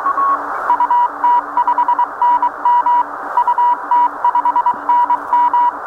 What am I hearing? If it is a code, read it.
Morse code, heard as a keyed tone, about 23 wpm, 970 Hz. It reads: UT5NM UT5NM
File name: attachments / UT5NM 2 RF 11.07.15